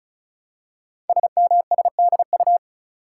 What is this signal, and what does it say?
Morse code, heard as a keyed tone, about 36 wpm, 700 Hz. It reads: SMSDU
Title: Code Groups - 5 Letters - 35wpm